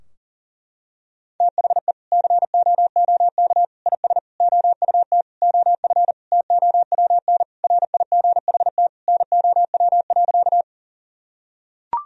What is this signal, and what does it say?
Morse code, heard as a keyed tone, about 40 wpm, 700 Hz. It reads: THE COOK IS OUT OF TOWN RIGHT NOW.
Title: Sentences from Top 1000 Words - Review Part 4 - 40wpm